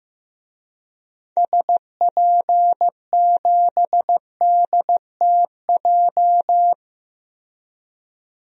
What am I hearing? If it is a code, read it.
Morse code, heard as a keyed tone, about 15 wpm, 700 Hz. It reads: SP7DTJ
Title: DXCC-100 Call Signs - 6 Characters - 15wpm